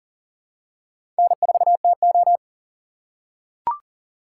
Morse code, heard as a keyed tone, about 40 wpm, 700 Hz. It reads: N4TO
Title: USA Call Signs - 4 Characters - 6x Character Spacing - 40wpm